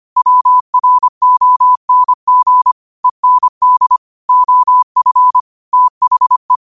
Morse code, heard as a keyed tone, about 25 wpm, 1000 Hz. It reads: WRONG END OF THE
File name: 25wpm